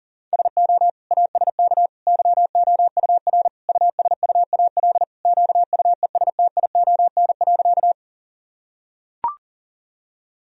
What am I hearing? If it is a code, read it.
Morse code, heard as a keyed tone, about 40 wpm, 700 Hz. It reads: SO ASK YOUR USUAL QUESTION.
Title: Sentences from Top 1000 Words - Review Part 2 - 40wpm